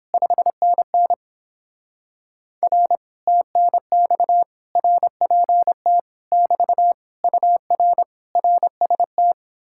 Morse code, heard as a keyed tone, about 29 wpm, 700 Hz. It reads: 5NN R TNX RPT <BT> UR RST